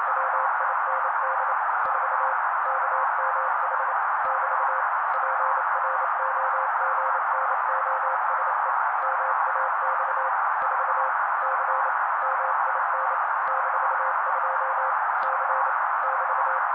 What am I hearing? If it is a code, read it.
Morse code, heard as a keyed tone, about 28 wpm, 530 Hz. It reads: WIND 4 KMH <BT> PROGNOSE MAX 4 C MIN -2 C <BT>
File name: daily / 27wpm noise